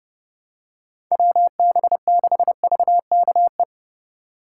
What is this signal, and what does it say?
Morse code, heard as a keyed tone, about 29 wpm, 700 Hz. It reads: WB6VKE